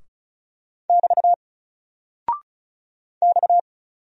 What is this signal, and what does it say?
Morse code, heard as a keyed tone, about 36 wpm, 700 Hz. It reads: <BT> X
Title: Single Character (Letter:Number:Punctuation:Prosign) 35wpm